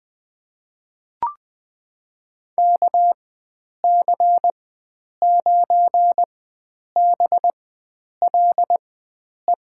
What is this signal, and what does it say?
Morse code, heard as a keyed tone, about 20 wpm, spaced as for 9 wpm, 700 Hz. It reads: KC9BLE
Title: USA Call Signs - 6 Characters - 4x Character Spacing - 20wpm